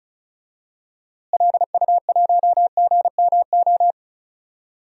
Morse code, heard as a keyed tone, about 36 wpm, 700 Hz. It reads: LU1GMO
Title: DXCC-100 Call Signs - 6 Characters - Repeated 2x - 35wpm